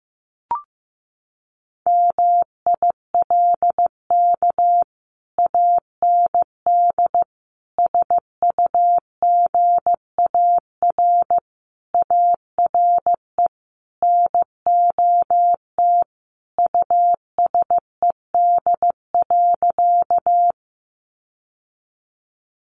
Morse code, heard as a keyed tone, about 15 wpm, 700 Hz. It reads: MILK AND SUGAR ARE NOT USED.
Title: Sentences from Top 1500 Words 15wpm